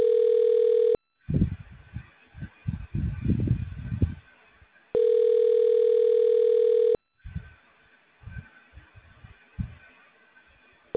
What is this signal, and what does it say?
Ringback tone (North America), 440 + 480 Hz.